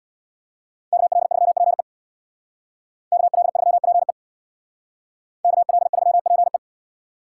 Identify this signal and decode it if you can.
Morse code, heard as a keyed tone, about 60 wpm, 700 Hz. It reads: KL3PE KL3PE KL3PE
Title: Speed Racing - USA 2x2 Call Signs 50wpm